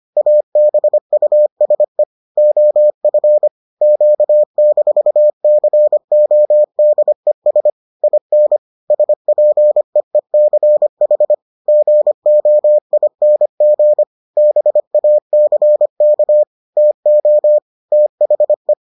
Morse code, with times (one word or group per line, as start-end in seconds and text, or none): 0.2-2.0 ABUSE
2.4-3.5 OF
3.8-7.7 Q-CODES
8.0-8.6 IN
8.9-11.3 SPEECH
11.7-14.0 GOING
14.4-16.4 BACK
16.8-17.6 TO
17.9-18.7 THE